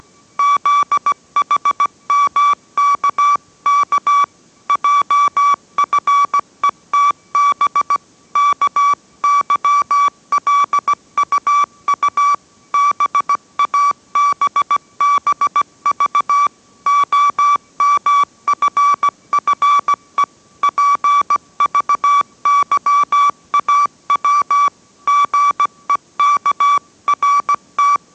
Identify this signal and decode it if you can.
Morse code, heard as a keyed tone, about 17 wpm, 1150 Hz. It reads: ZHMKK JFETB KYLUU BABBV OMFFE PVYAW GEKRT